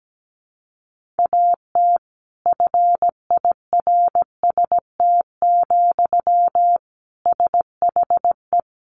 Morse code, heard as a keyed tone, about 17 wpm, 700 Hz. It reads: AT FIRST, SHE
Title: Sentences from Top 1400 Words 17wpm